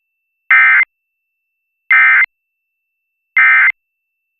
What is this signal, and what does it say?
Emergency Alert System (SAME) tones, heard as harsh buzzing bursts. It is heard from 0.5 to 3.7 s.